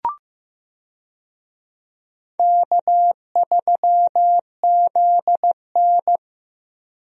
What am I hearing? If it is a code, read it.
Morse code, heard as a keyed tone, about 15 wpm, 700 Hz. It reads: K3ZN
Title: USA Call Signs - 4 Characters - Repeated 2x - 15wpm